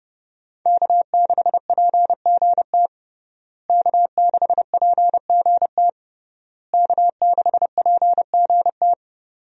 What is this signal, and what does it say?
Morse code, heard as a keyed tone, about 30 wpm, 700 Hz. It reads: K6PGT K6PGT K6PGT
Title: USA Call Signs - 5 Characters - Repeated 3x - 30wpm